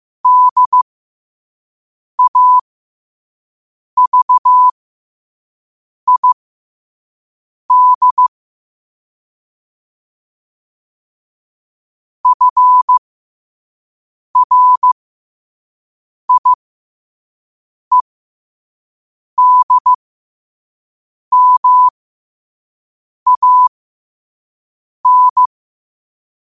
Morse code, heard as a keyed tone, about 15 wpm, spaced as for 5 wpm, 1000 Hz. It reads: DAVID FRIEDMAN
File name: 5wpm